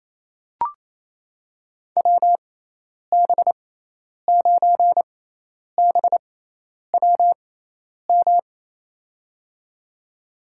Morse code, heard as a keyed tone, about 28 wpm, spaced as for 10 wpm, 700 Hz. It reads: WB9BWM
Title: USA Call Signs - 6 Characters - 6x Character Spacing - 28wpm